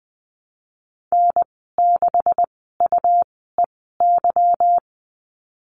Morse code, heard as a keyed tone, about 20 wpm, spaced as for 15 wpm, 700 Hz. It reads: N6UEY
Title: 154 Lesson 25g - Letter p - Call Signs 20wpm